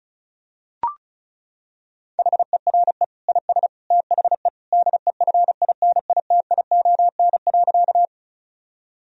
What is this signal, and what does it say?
Morse code, heard as a keyed tone, about 36 wpm, 700 Hz. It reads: HERE IS THE DEFINITION.
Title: Sentences from Top 1000 Words - Review Part 4 - 35wpm